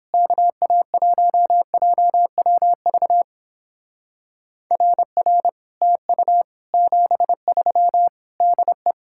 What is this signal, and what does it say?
Morse code, heard as a keyed tone, about 31 wpm, 700 Hz. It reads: KA1JWV RR TU 73 DE